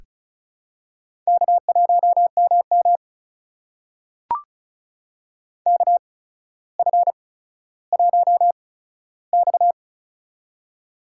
Morse code, heard as a keyed tone, about 36 wpm, spaced as for 15 wpm, 700 Hz. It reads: K1MM KF1X
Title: USA Call Signs - 4 Characters - 8x Character Spacing - 35wpm